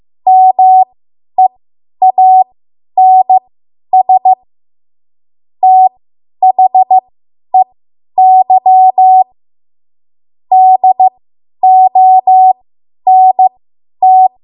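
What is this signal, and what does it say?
Morse code, heard as a keyed tone, about 15 wpm, spaced as for 10 wpm, 750 Hz. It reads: MEANS THEY DONT
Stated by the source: Morse code practice run at 10 WPM from a code practice archive.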